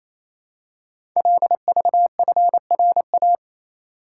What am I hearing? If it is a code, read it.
Morse code, heard as a keyed tone, about 28 wpm, 700 Hz. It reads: LVFRA
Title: Code Groups - 5 Letters - 28wpm